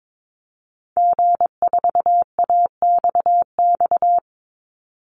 Morse code, heard as a keyed tone, about 22 wpm, 700 Hz. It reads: G4AXX
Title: Call Signs - 22wpm